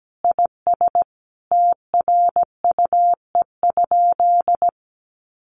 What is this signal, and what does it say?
Morse code, heard as a keyed tone, about 17 wpm, 700 Hz. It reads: IS TRUE?
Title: Sentences from Top 1400 Words 17wpm